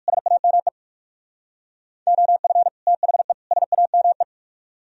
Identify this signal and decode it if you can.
Morse code, heard as a keyed tone, about 58 wpm, 700 Hz. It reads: SAME OF THE SAME